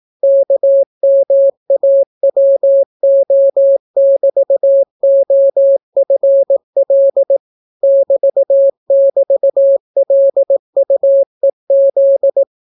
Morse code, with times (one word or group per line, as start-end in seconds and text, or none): 0.2-7.4 KMAWO<BT>OFL
7.8-12.4 <BT><BT>LUEZ